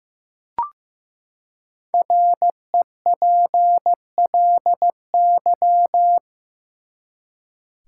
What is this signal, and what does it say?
Morse code, heard as a keyed tone, about 15 wpm, 700 Hz. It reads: REPLY
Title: Five Letter Words - 15wpm